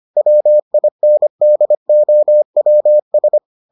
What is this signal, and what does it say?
Morse code, heard as a keyed tone, about 25 wpm, 600 Hz. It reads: WINDOWS